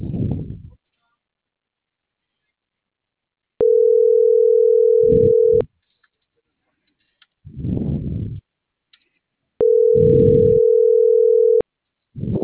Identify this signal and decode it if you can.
Ringback tone (North America), 440 + 480 Hz.